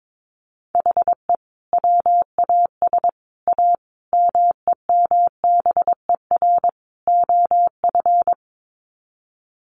Morse code, heard as a keyed tone, about 22 wpm, 700 Hz. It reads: HE WAS A MEMBER OF
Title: Sets of 5 Words Encore2 22wpm